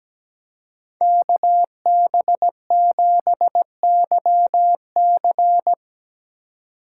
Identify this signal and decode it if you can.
Morse code, heard as a keyed tone, about 19 wpm, 700 Hz. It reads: KB7YC
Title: Speed Racing - USA 2x2 Call Signs 17wpm